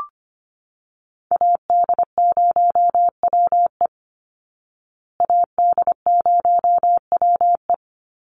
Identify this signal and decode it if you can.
Morse code, heard as a keyed tone, about 25 wpm, 700 Hz. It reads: AD0WE AD0WE
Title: Call Signs Repeated 3x - 25wpm